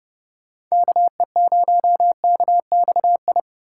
Morse code, heard as a keyed tone, about 35 wpm, 700 Hz. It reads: KE0KXI
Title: Speed Racing - USA 2x3 Call Signs 30wpm